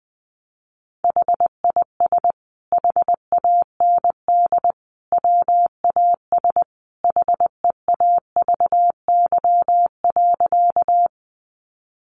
Morse code, heard as a keyed tone, about 19 wpm, 700 Hz. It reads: HIS HAND WAS HEAVY.
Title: Sentences from Top 1100 Words 20wpm